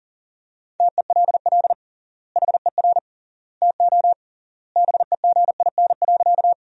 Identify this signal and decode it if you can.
Morse code, heard as a keyed tone, about 40 wpm, 700 Hz. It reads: TELL HER TO BEGIN.
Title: 3x Word-Spacing - Sentences from Top 300 Words - 40wpm